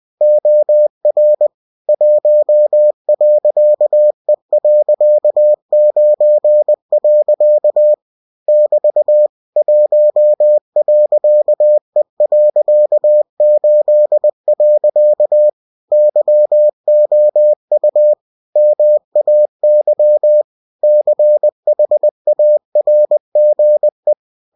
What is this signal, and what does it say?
Morse code, heard as a keyed tone, about 20 wpm, 600 Hz. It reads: OR 1.E.9. <BT> 1.E.8. YOU MAY CHARGE
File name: cw-books / winnie-the-pooh / winnie-20wpm-page-0116